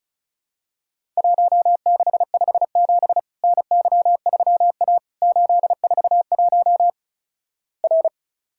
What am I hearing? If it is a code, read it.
Morse code, heard as a keyed tone, about 36 wpm, 700 Hz. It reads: 1657 NY3A 841 R
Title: Worked All Europe DX Contest - QTC Exchange - 35wpm